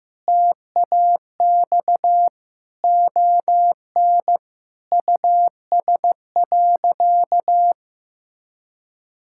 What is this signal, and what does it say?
Morse code, heard as a keyed tone, about 15 wpm, 700 Hz. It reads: TAX ON US.
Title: Sentences from Top 1000 Words - Review Part 5 - 15wpm